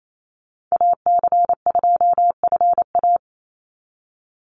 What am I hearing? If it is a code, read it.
Morse code, heard as a keyed tone, about 28 wpm, 700 Hz. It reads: AC2FA
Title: Call Signs - 28wpm